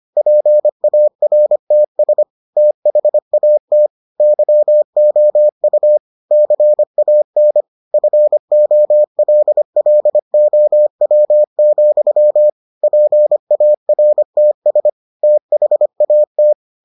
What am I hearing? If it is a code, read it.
Morse code, heard as a keyed tone, about 25 wpm, 600 Hz. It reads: PARTS THAT YOU CAN FOLLOW, PARTS THAT